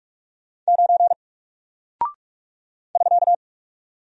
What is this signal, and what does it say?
Morse code, heard as a keyed tone, about 44 wpm, 700 Hz. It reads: 9 <SK>
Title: Single Character (Letter:Number:Punctuation:Prosign) Alt 45wpm